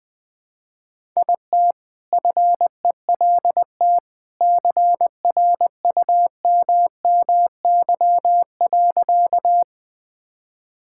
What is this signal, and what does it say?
Morse code, heard as a keyed tone, about 20 wpm, 700 Hz. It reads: IT FELT CRUMMY.
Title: All-Age Appropriate Jokes - 20wpm